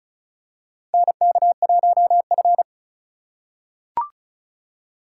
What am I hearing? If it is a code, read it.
Morse code, heard as a keyed tone, about 36 wpm, 700 Hz. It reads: NK1F
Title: USA Call Signs - 4 Characters - 2x Character Spacing - 35wpm